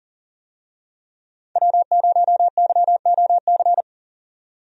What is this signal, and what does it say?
Morse code, heard as a keyed tone, about 40 wpm, 700 Hz. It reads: W0YOC
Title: USA Call Signs - 5 Characters - 40wpm